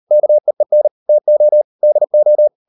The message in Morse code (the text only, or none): KEEN TO DO